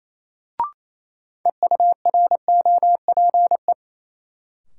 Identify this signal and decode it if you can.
Morse code, heard as a keyed tone, about 28 wpm, 700 Hz. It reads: EUROPE
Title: Sets of 5 Words Encore - New Word Drill - 28wpm